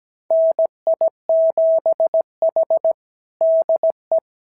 Morse code, heard as a keyed tone, about 17 wpm, 660 Hz. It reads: NI7H DE
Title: SOTA QSO Practice 17wpm